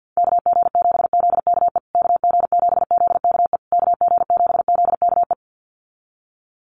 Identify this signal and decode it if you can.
Morse code, heard as a keyed tone, about 50 wpm, 700 Hz. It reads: KG7ZKE KG7ZKE KG7ZKE
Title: USA Call Signs Repeated 3x 50wpm